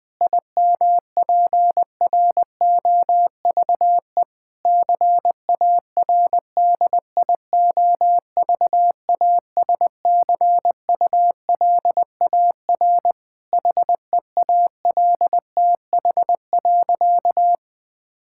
Morse code, heard as a keyed tone, about 21 wpm, 700 Hz. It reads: IMPROVE CARDIOVASCULAR HEALTH.
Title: Difficult Sentences with Hints - 20wpm